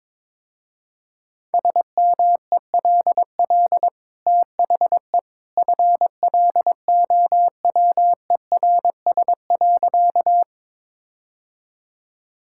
Morse code, heard as a keyed tone, about 21 wpm, 700 Hz. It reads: SMELL THE FLOWERS.
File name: practice / 22wpm / Easy Sentences with Hints - 22wpm